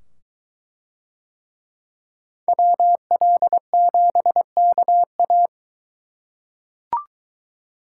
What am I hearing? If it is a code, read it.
Morse code, heard as a keyed tone, about 22 wpm, 700 Hz. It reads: WL7KA